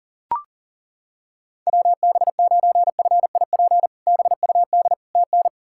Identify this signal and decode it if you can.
Morse code, heard as a keyed tone, about 40 wpm, 700 Hz. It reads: WB9FIP BUD TN
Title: K1USN SST QSO Practice - 40wpm